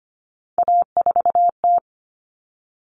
Morse code, heard as a keyed tone, about 25 wpm, 700 Hz. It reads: A4T
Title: All Cut Numbers - 3-Digit 25wpm